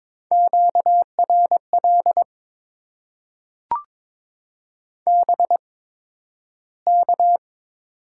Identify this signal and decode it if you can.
Morse code, heard as a keyed tone, about 22 wpm, spaced as for 12 wpm, 690 Hz. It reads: QRL BK
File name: Top 90 QSO Elements - 8x Character Spacing - 22wpm